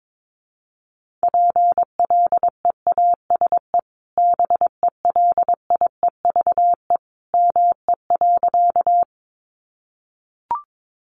Morse code, heard as a keyed tone, about 22 wpm, 700 Hz. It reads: PLEASE BELIEVE ME.
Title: Sentences from Top 1500 Words 22wpm